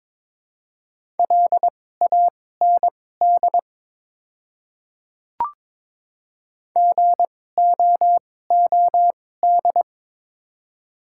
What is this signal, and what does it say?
Morse code, heard as a keyed tone, about 22 wpm, spaced as for 16 wpm, 700 Hz. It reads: LAND GOOD
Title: 2x Character Spacing - Top 200 Words - 22wpm